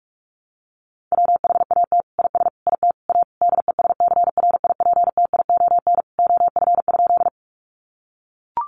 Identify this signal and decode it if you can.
Morse code, heard as a keyed tone, about 44 wpm, 700 Hz. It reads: WHAT IS IT A DESCRIPTION OF?
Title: Sentences from Top 1300 Words 45wpm